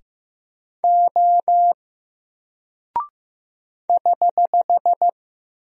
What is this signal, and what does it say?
Morse code, heard as a keyed tone, about 15 wpm, 700 Hz. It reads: O <HH>
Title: Single Character (Letter:Number:Punctuation:Prosign) Alt 15wpm